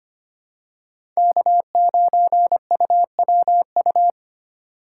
Morse code, heard as a keyed tone, about 25 wpm, 700 Hz. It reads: K9UWU
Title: USA Call Signs - 5 Characters - 3x Character Spacing - 25wpm